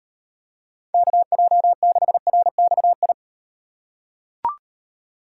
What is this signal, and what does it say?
Morse code, heard as a keyed tone, about 38 wpm, 700 Hz. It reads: KJ6RXI